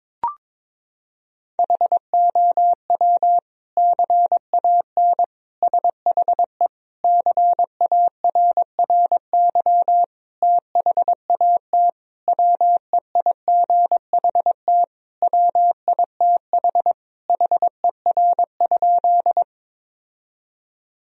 Morse code, heard as a keyed tone, about 22 wpm, 700 Hz. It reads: HOW CAN SHE CARRY THAT WEIGHT WITH HER?
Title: Sentences from Top 1000 Words - Review Part 3 - 22wpm